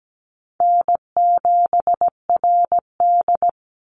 Morse code, heard as a keyed tone, about 17 wpm, 700 Hz. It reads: N7RD
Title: Easy Call Signs Repeated 2x - 17wpm